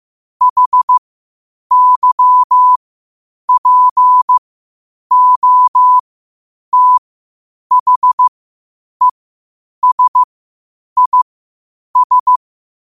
Morse code, heard as a keyed tone, about 15 wpm, spaced as for 8 wpm, 1000 Hz. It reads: HYPOTHESIS
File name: qotdfiles / 8wpm